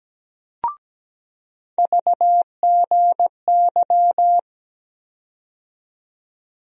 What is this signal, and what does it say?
Morse code, heard as a keyed tone, about 17 wpm, 700 Hz. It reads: VGY
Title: USA Call Sign Postfixes - 3 Letters 17wpm